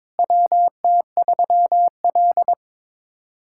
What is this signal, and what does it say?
Morse code, heard as a keyed tone, about 22 wpm, 700 Hz. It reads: WT3L
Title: USA Callsign DE Callsign - 4 Characters - 22wpm